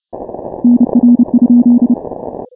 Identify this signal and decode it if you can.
Morse code, heard as a keyed tone, about 30 wpm, 240 Hz. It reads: NR?
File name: final / W1UJ-168-140-MorseRunner